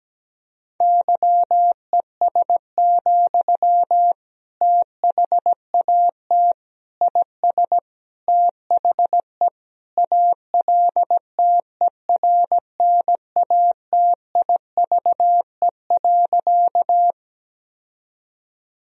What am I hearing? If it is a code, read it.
Morse code, heard as a keyed tone, about 17 wpm, 700 Hz. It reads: YES, THAT IS THE ALTERNATIVE.
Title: Sentences from Top 1000 Words - Review Part 4 - 17wpm